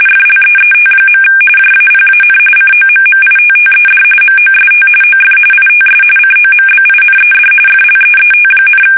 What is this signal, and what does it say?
RTTY, about 45.45 baud. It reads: FLEET. JAPAN COMMITTED MOST OF ITS COMBINED FLEET,
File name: fsk850 / battleshipnj